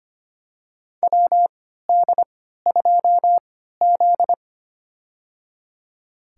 Morse code, heard as a keyed tone, about 25 wpm, spaced as for 14 wpm, 700 Hz. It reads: WD2Z
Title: USA Call Signs - 4 Characters - 3x Character Spacing - 25wpm